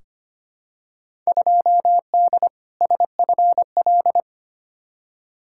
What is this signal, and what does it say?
Morse code, heard as a keyed tone, about 25 wpm, 700 Hz. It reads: 2D SFL
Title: Field Day Exchanges Encore - 25wpm